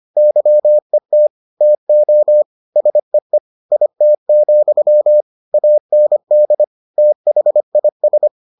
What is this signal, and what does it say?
Morse code, heard as a keyed tone, about 25 wpm, 600 Hz. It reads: YET TO SEE IT, AND THIS